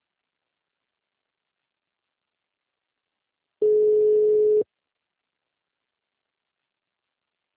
Ringback tone (Europe, 425 Hz).